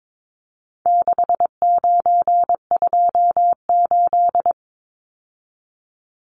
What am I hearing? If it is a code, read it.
Morse code, heard as a keyed tone, about 22 wpm, 700 Hz. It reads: 6928
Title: Numbers - 4-Digit 22wpm